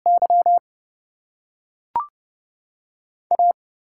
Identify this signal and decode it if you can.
Morse code, heard as a keyed tone, about 29 wpm, spaced as for 3 wpm, 700 Hz. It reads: YA
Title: USA Call Signs - 5 Characters - 8x Character Spacing - 30wpm